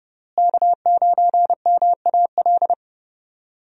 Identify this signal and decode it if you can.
Morse code, heard as a keyed tone, about 30 wpm, 700 Hz. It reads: K9MAL
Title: USA Call Signs - 5 Characters - Repeated 3x - 30wpm